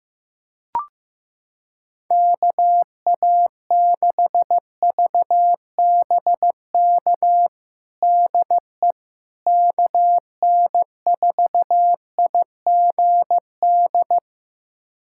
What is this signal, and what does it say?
Morse code, heard as a keyed tone, about 15 wpm, 700 Hz. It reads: KA6VBK DE KN4IGD
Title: USA Callsign DE Callsign 15wpm